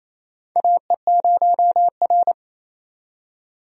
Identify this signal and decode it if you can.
Morse code, heard as a keyed tone, about 28 wpm, 700 Hz. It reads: AE0R
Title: USA Call Signs - 4 Characters - 28wpm